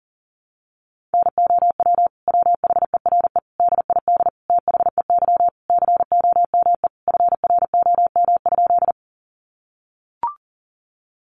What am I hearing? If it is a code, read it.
Morse code, heard as a keyed tone, about 37 wpm, 700 Hz. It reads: NOW WHERE DID THEY COME FROM?
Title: Sentences from Top 200 Words 40wpm